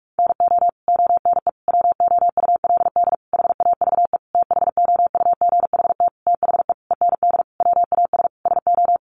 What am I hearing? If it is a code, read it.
Morse code, heard as a keyed tone, about 44 wpm, 700 Hz. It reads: NO ONE WOULD HAVE THOUGHT THE END WAS SO
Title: Sentences from Top 300 Words 45wpm